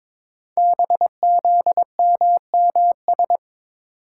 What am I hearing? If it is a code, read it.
Morse code, heard as a keyed tone, about 22 wpm, 700 Hz. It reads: BZMMS
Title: Code Groups - 5 Letters - 22wpm